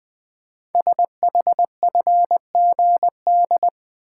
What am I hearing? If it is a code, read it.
Morse code, heard as a keyed tone, about 20 wpm, 700 Hz. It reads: SHFGD
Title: Code Groups - 5 Letters - 20wpm